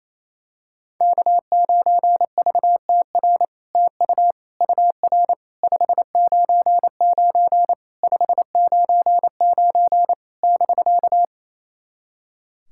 Morse code, heard as a keyed tone, about 27 wpm, 700 Hz. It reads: K9VTR TU UR 599 599 <BK>
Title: SOTA QSO Practice 28wpm